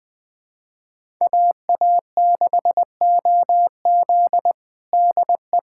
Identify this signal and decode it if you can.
Morse code, heard as a keyed tone, about 20 wpm, 700 Hz. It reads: AA6OZ DE
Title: USA Callsign DE Callsign 20wpm